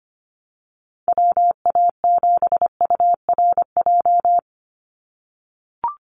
Morse code, heard as a keyed tone, about 25 wpm, 700 Hz. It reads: WA7URJ
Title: USA Call Signs 25wpm